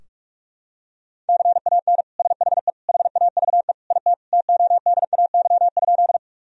Morse code, heard as a keyed tone, about 42 wpm, 700 Hz. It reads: CAN SHE HAVE IT TODAY?
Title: Sentences from Top 1000 Words - Review Part 4 - 45wpm